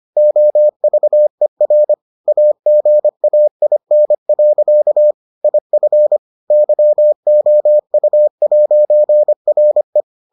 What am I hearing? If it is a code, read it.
Morse code, heard as a keyed tone, about 25 wpm, 600 Hz. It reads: OVER AGAIN. IF YOU'RE